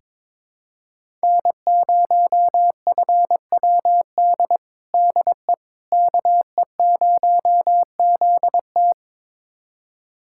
Morse code, heard as a keyed tone, about 21 wpm, 700 Hz. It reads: N0FWD DE KE0ZT
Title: USA Callsign DE Callsign - 5 Characters - 22wpm